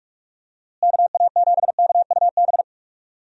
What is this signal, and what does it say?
Morse code, heard as a keyed tone, about 48 wpm, 700 Hz. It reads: KA7KUB
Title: USA Call Signs - 6 Characters - Repeated 2x - 45wpm